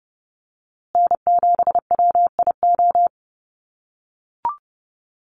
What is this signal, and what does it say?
Morse code, heard as a keyed tone, about 30 wpm, 700 Hz. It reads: N7WIO
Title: USA Call Signs Repeated 2x 30wpm